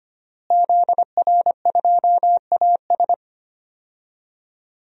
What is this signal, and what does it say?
Morse code, heard as a keyed tone, about 25 wpm, 700 Hz. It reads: ZR2AS
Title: DXCC-100 Call Signs - 5 Characters - Repeated 3x - 25wpm